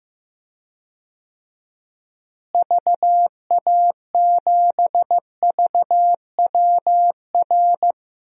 Morse code, heard as a keyed tone, about 15 wpm, 700 Hz. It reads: VA7VWR
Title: DXCC-100 Call Signs - 6 Characters - Repeated 2x - 15wpm